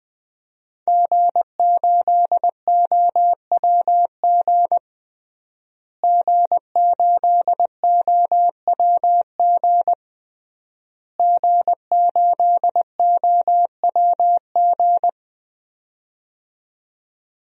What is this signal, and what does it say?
Morse code, heard as a keyed tone, about 20 wpm, 700 Hz. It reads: G8OWG G8OWG G8OWG
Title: DXCC-100 Call Signs - 5 Characters - Repeated 3x - 20wpm